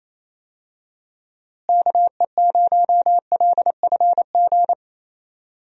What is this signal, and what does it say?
Morse code, heard as a keyed tone, about 28 wpm, 700 Hz. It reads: KE0LFG
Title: USA Call Signs - 6 Characters - Repeated 2x - 28wpm